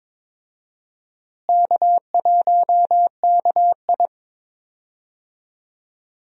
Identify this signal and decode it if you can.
Morse code, heard as a keyed tone, about 22 wpm, 700 Hz. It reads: K1KI
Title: USA Call Signs - 4 Characters - Repeated 2x - 22wpm